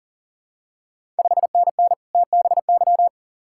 Morse code, heard as a keyed tone, about 39 wpm, 700 Hz. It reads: 5NN TBY
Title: Ontario QSO Party Exchange 40wpm